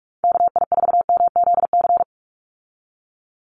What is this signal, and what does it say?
Morse code, heard as a keyed tone, about 44 wpm, 700 Hz. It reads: KI4MZC
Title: Call Signs Repeated 2x - 45wpm